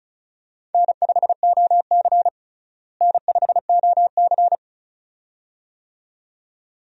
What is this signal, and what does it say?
Morse code, heard as a keyed tone, about 36 wpm, 700 Hz. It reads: N5OC N5OC
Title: USA Call Signs - 4 Characters - Repeated 2x - 35wpm